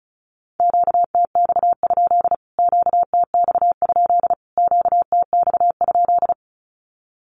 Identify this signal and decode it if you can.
Morse code, heard as a keyed tone, about 34 wpm, 700 Hz. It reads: QTX? QTX? QTX?
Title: Q-Codes Repeated 3x 35wpm